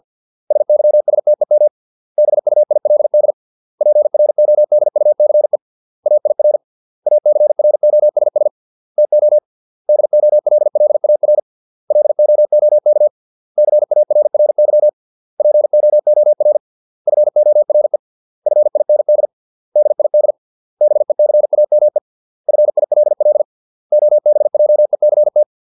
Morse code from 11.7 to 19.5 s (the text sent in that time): LOOK CARRY POOR FORE FIND